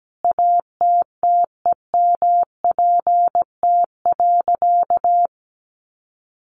Morse code, heard as a keyed tone, about 17 wpm, 700 Hz. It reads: ATTEMPT.
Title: Sentences from Top 1000 Words 17wpm